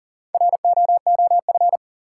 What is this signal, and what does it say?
Morse code, heard as a keyed tone, about 40 wpm, 700 Hz. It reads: ROOF
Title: Four Letter Words - 40wpm